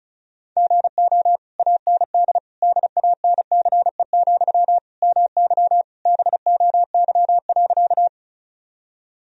Morse code, heard as a keyed tone, about 34 wpm, 700 Hz. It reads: GO AND DANCE, MY BOY.
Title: Sentences from Top 1000 Words - Review Part 4 - 35wpm